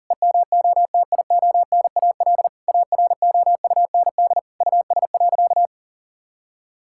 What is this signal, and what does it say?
Morse code, heard as a keyed tone, about 40 wpm, 700 Hz. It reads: EMOTIONAL AROUND US.
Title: Sentences from Top 1000 Words - Review Part 5 - 40wpm